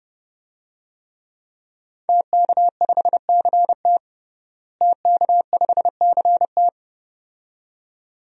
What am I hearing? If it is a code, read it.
Morse code, heard as a keyed tone, about 30 wpm, 700 Hz. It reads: TK5CT TK5CT
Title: DXCC-100 Call Signs - 5 Characters - Repeated 2x - 30wpm